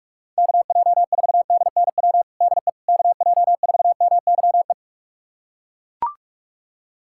Morse code, heard as a keyed tone, about 41 wpm, 700 Hz. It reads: KJ4DNW DE KJ4MYE
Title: USA Callsign DE Callsign - 6 Characters - 45wpm